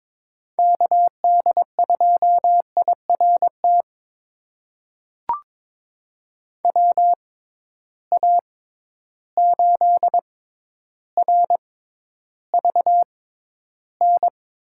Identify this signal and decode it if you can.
Morse code, heard as a keyed tone, about 22 wpm, spaced as for 11 wpm, 700 Hz. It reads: KD2IRT WA8RVN
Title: USA Call Signs - 6 Characters - 6x Character Spacing - 22wpm